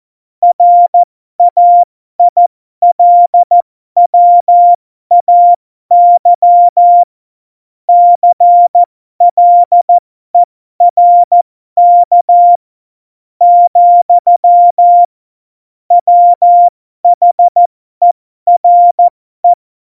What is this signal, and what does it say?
Morse code, heard as a keyed tone, about 14 wpm, spaced as for 12 wpm, 700 Hz. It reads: RAILWAY CLERK , WHERE
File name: Morsekurs / Witze